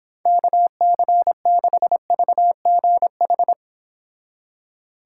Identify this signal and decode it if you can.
Morse code, heard as a keyed tone, about 29 wpm, 700 Hz. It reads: KC6VGH